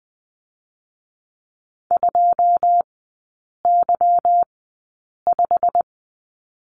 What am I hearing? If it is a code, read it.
Morse code, heard as a keyed tone, about 20 wpm, 690 Hz. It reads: 2 Y 5
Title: 145 Lesson 24e - Letter y - Three Characters 20wpm